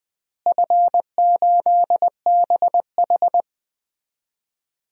Morse code, heard as a keyed tone, about 21 wpm, 700 Hz. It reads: F8BH
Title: DXCC-100 Call Signs - 4 Characters - Repeated 3x - 20wpm